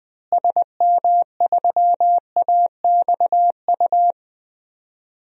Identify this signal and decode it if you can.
Morse code, heard as a keyed tone, about 20 wpm, 700 Hz. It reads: SM3AXU
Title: DXCC-100 Call Signs - 6 Characters - Repeated 3x - 20wpm